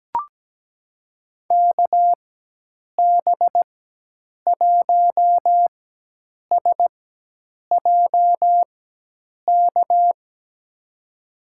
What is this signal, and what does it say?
Morse code, heard as a keyed tone, about 17 wpm, spaced as for 8 wpm, 700 Hz. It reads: KB1SJK
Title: USA Call Signs - 6 Characters - 4x Character Spacing - 17wpm